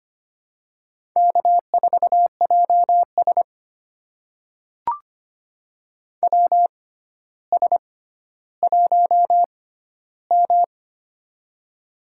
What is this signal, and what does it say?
Morse code, heard as a keyed tone, about 25 wpm, spaced as for 13 wpm, 700 Hz. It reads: K4JS WS1M
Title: USA Call Signs - 4 Characters - 6x Character Spacing - 25wpm